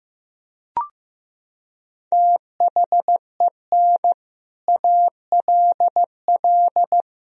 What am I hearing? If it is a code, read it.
Morse code, heard as a keyed tone, about 16 wpm, 700 Hz. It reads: THEN ALL
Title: Sentences from Top 500 Words and Abbreviations 15wpm